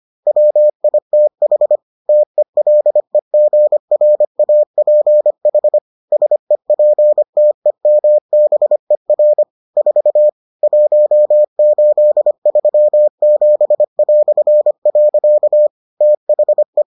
Morse code, heard as a keyed tone, about 25 wpm, 600 Hz. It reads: WITH TELEGRAPH SEPTEMBER 4 1837". THE